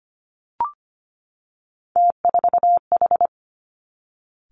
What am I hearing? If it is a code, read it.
Morse code, heard as a keyed tone, about 25 wpm, 700 Hz. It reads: T4H
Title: Three Challenging Letter-Number Combinations 25wpm